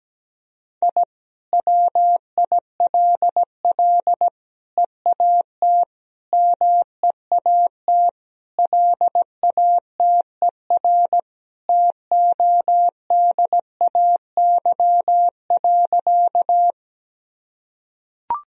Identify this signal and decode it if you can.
Morse code, heard as a keyed tone, about 17 wpm, 700 Hz. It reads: I WILL EAT MEAT LATER TODAY.
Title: Sentences from Top 1000 Words - Review Part 3 - 17wpm